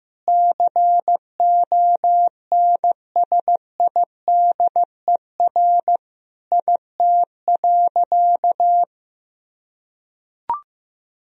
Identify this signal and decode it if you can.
Morse code, heard as a keyed tone, about 15 wpm, 700 Hz. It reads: CONSIDER IT.
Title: Sentences from Top 1000 Words - Review Part 4 - 15wpm